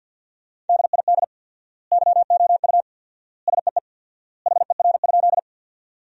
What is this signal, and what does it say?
Morse code, heard as a keyed tone, about 48 wpm, 700 Hz. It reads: DID YOU SEE HER?
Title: 4x Word-Spacing - Sentences from Top 100 Words - 50wpm